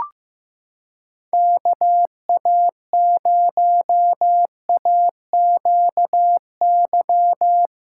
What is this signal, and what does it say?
Morse code, heard as a keyed tone, about 15 wpm, 700 Hz. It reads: KA0AQY
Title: USA Callsign DE Callsign - 6 Characters - 15wpm